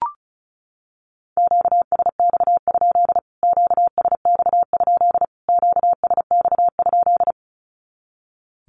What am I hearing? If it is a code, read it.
Morse code, heard as a keyed tone, about 34 wpm, 700 Hz. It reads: QSX? QSX? QSX?
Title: Q-Codes Repeated 3x 35wpm